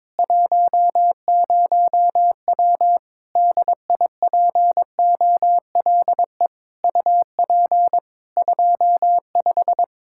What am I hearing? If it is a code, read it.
Morse code, heard as a keyed tone, about 22 wpm, 700 Hz. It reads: 10W DIPOLE UP 25
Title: K1USN SST QSO Practice 22wpm